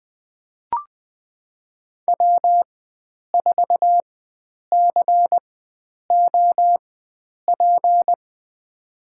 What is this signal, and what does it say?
Morse code, heard as a keyed tone, about 20 wpm, spaced as for 9 wpm, 700 Hz. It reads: W4COP
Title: USA Call Signs - 5 Characters - 4x Character Spacing - 20wpm